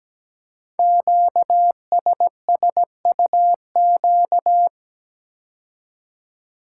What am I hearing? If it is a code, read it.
Morse code, heard as a keyed tone, about 17 wpm, 700 Hz. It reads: QSSUQ
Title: Code Groups - 5 Letters - 17wpm